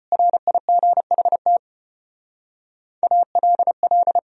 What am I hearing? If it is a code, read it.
Morse code, heard as a keyed tone, about 32 wpm, 700 Hz. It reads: RIGHT ALL